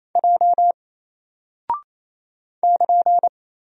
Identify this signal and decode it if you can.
Morse code, heard as a keyed tone, about 27 wpm, 700 Hz. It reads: J <KN>
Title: Single Character (Letter:Number:Punctuation:Prosign) 28wpm